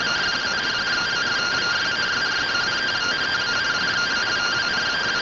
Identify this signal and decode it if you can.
Two-tone teleprinter signal, about 170 Hz shift.